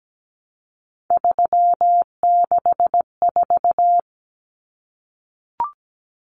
Morse code, heard as a keyed tone, about 17 wpm, 700 Hz. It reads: 364
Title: Common Cut Numbers - 3-Digit 17wpm